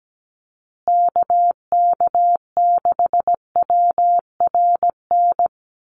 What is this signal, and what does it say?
Morse code, heard as a keyed tone, about 17 wpm, 700 Hz. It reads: KK6WRN
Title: USA Call Signs Repeated 2x 17wpm